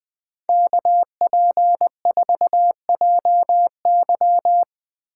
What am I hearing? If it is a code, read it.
Morse code, heard as a keyed tone, about 20 wpm, 700 Hz. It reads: KP4JY
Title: Speed Racing - USA Call Signs 17wpm